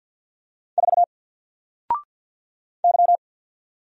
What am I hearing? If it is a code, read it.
Morse code, heard as a keyed tone, about 50 wpm, 700 Hz. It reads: 4 Y
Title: Single Letter-Number 50wpm